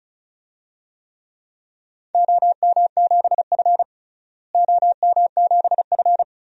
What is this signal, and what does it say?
Morse code, heard as a keyed tone, about 34 wpm, 700 Hz. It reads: OM7F OM7F
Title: DXCC-100 Call Signs - 4 Characters - Repeated 2x - 35wpm